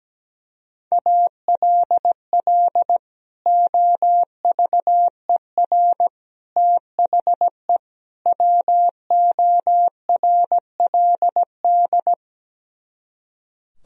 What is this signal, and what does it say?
Morse code, heard as a keyed tone, about 17 wpm, 700 Hz. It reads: ALL OVER THE WORLD
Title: Speed-Racing - Sets of 4 Words 17wpm